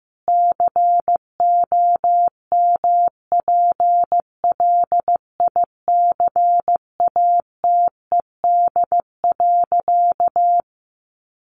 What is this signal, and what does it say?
Morse code, heard as a keyed tone, about 15 wpm, 700 Hz. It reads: COMPLICATED.
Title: Sentences from Top 1200 Words 15wpm